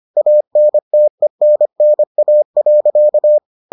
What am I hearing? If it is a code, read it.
Morse code, heard as a keyed tone, about 25 wpm, 600 Hz. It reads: ANTENNA.